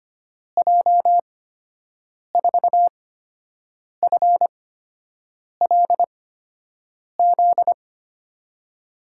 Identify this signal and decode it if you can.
Morse code, heard as a keyed tone, about 25 wpm, spaced as for 7 wpm, 700 Hz. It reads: J4FLZ
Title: USA Call Signs - 6 Characters - 8x Character Spacing - 25wpm